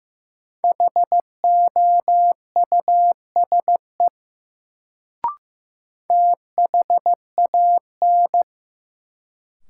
Morse code, heard as a keyed tone, about 15 wpm, 700 Hz. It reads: HOUSE THAN
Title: Sets of 2 Words - New Word Drill - 15wpm